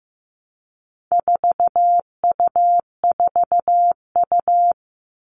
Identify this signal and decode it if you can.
Morse code, heard as a keyed tone, about 15 wpm, 700 Hz. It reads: 4U4U
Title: All Cut Numbers - 4-Digit 15wpm